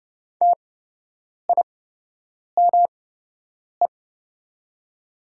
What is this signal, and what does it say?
Morse code, heard as a keyed tone, about 29 wpm, spaced as for 8 wpm, 700 Hz. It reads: TIME
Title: 8x Character Spacing - Top 200 Words - 30wpm